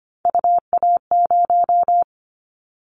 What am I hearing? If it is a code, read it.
Morse code, heard as a keyed tone, about 25 wpm, 700 Hz. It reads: UA0
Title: Callsign Prefixes for Top 50 DXCC - 25wpm